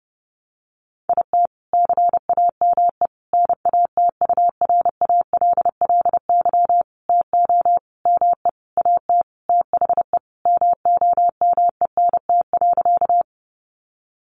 Morse code, heard as a keyed tone, about 30 wpm, 700 Hz. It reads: IT CAME NATURALLY TO ME AT THE MOMENT.
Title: Sentences from Top 1300 Words 30wpm